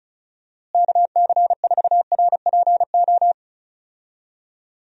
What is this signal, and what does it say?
Morse code, heard as a keyed tone, about 34 wpm, 700 Hz. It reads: KC4RPO
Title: USA Call Signs - 6 Characters - Repeated 3x - 35wpm